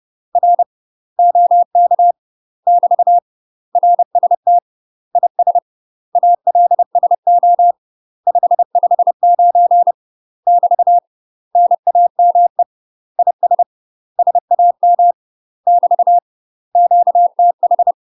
Morse code, heard as a keyed tone, about 30 wpm, 700 Hz. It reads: R OK <BT> RST IS ALSO 559 <BT> NAME IS SAM <BT> QTH
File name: UK MASTER QSOs TRAINING / 30WPM / UK MASTER QSOs 0003